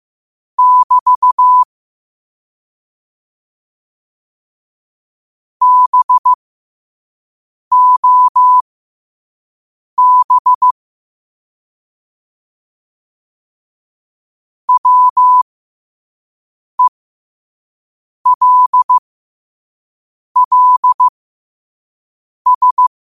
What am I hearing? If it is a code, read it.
Morse code, heard as a keyed tone, about 15 wpm, spaced as for 5 wpm, 1000 Hz. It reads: <BT> BOB WELLS
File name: qotdfiles / 5wpm